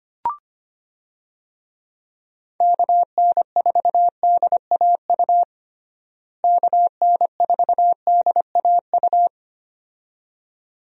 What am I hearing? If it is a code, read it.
Morse code, heard as a keyed tone, about 25 wpm, 700 Hz. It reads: KN4DAU KN4DAU
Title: USA Call Signs - 6 Characters - Repeated 2x - 25wpm